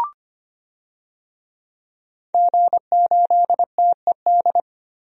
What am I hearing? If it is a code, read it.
Morse code, heard as a keyed tone, about 25 wpm, 700 Hz. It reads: G8TED
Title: DXCC-100 Call Signs - 5 Characters - 25wpm